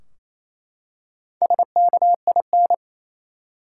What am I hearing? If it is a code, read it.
Morse code, heard as a keyed tone, about 27 wpm, 700 Hz. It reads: SKIN
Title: Four Letter Words - 28wpm